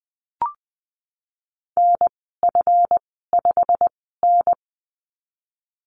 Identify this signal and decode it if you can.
Morse code, heard as a keyed tone, about 20 wpm, spaced as for 15 wpm, 700 Hz. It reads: NF5N
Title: 193 Lesson 31g - Letter v - Call Signs 20wpm